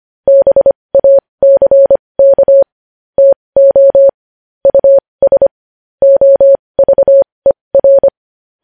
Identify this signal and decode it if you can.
Morse code, heard as a keyed tone, about 25 wpm, spaced as for 20 wpm, 550 Hz. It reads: BACK TO US OVER
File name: podcast / News headlines-1x-25x20